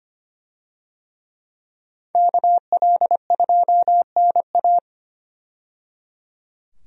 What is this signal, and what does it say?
Morse code, heard as a keyed tone, about 25 wpm, 700 Hz. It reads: KL2NA
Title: USA Call Signs - 5 Characters - 25wpm